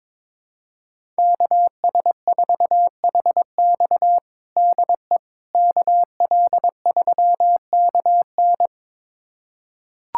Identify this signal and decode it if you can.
Morse code, heard as a keyed tone, about 22 wpm, 700 Hz. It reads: KS4HX DE KL3KN
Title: USA Callsign DE Callsign - 5 Characters - 22wpm